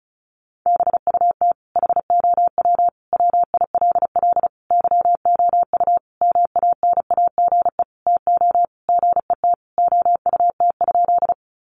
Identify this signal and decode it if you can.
Morse code, heard as a keyed tone, about 34 wpm, 700 Hz. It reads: BUT HOW WILL YOU MANAGE TO GET OUT?
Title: Sentences from Top 1400 Words 35wpm